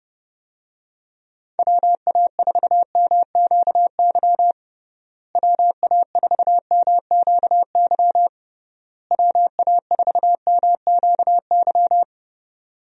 Morse code, heard as a keyed tone, about 31 wpm, 700 Hz. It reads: WA4MQY WA4MQY WA4MQY
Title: USA Call Signs - 6 Characters - Repeated 3x - 30wpm